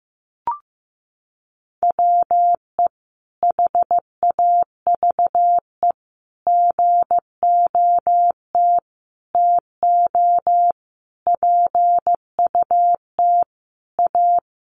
Morse code, heard as a keyed tone, about 15 wpm, 700 Hz. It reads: WE HAVE GOT TO PUT A
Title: Sentences from Top 1200 Words 15wpm